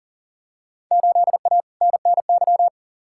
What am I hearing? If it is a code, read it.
Morse code, heard as a keyed tone, about 40 wpm, 700 Hz. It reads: 8A NNY
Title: Field Day Exchanges Encore - 40wpm